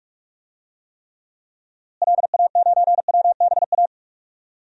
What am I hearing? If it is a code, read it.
Morse code, heard as a keyed tone, about 41 wpm, 700 Hz. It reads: LA9WBA
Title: DXCC-100 Call Signs - 6 Characters - Repeated 3x - 45wpm